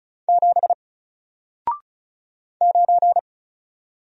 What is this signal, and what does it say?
Morse code, heard as a keyed tone, about 36 wpm, 700 Hz. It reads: 7 9
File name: Single-Letter-Number-35wpm